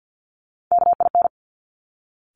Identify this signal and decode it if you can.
Morse code, heard as a keyed tone, about 50 wpm, 700 Hz. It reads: KIN
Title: Top Three Letter Combinations in Words 50wpm